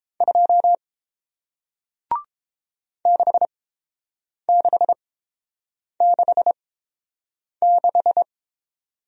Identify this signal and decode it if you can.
Morse code, heard as a keyed tone, about 30 wpm, spaced as for 7 wpm, 700 Hz. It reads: 2 6666